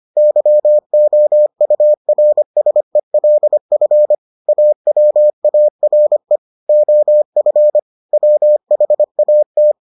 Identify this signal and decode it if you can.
Morse code, heard as a keyed tone, about 25 wpm, 600 Hz. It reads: YOURSELF AWARE OF WHAT